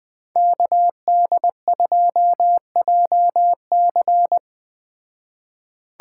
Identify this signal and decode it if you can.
Morse code, heard as a keyed tone, about 20 wpm, 700 Hz. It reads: KD2JC
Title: Speed Racing - USA 2x2 Call Signs 17wpm